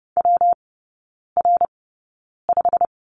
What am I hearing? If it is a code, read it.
Morse code, heard as a keyed tone, about 30 wpm, spaced as for 9 wpm, 700 Hz. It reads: WR5
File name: Lesson 19e - Letter w - Three Characters 30wpm